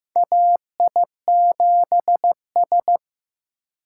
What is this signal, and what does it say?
Morse code, heard as a keyed tone, about 15 wpm, 700 Hz. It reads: AI7S
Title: USA Call Signs - 4 Characters - 3x Character Spacing - 15wpm